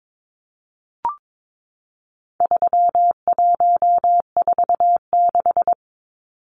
Morse code, heard as a keyed tone, about 22 wpm, 700 Hz. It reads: 3146
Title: Numbers - 4-Digit 22wpm